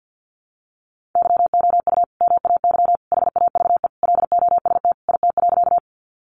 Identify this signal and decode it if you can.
Morse code, heard as a keyed tone, about 50 wpm, 700 Hz. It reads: YOU MAY HAVE LOST IT.
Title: Sentences from Top 1400 Words 50wpm